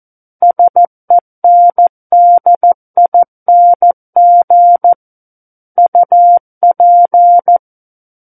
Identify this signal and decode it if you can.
Morse code, heard as a keyed tone, about 14 wpm, 700 Hz. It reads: SENDING UP
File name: Book One / 05-Chapter 5 - 14 WPM